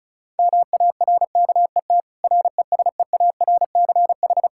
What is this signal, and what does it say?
Morse code, heard as a keyed tone, about 34 wpm, 700 Hz. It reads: MARKET RESEARCH
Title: Practice QSO - Exchange 3 - More Info - 35wpm